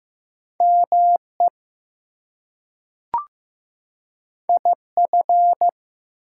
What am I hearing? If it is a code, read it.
Morse code, heard as a keyed tone, about 15 wpm, 700 Hz. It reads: ME IF
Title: Two Letter Words - 15wpm